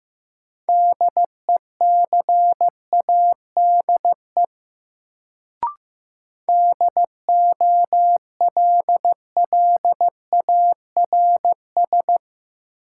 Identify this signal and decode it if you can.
Morse code, heard as a keyed tone, about 15 wpm, 700 Hz. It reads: DECADE DOLLARS
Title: Sets of 5 Words - New Word Drill - 15wpm